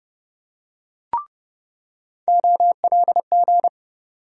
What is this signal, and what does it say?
Morse code, heard as a keyed tone, about 30 wpm, 700 Hz. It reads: OLG
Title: USA Call Sign Postfixes - 3 Letters 30wpm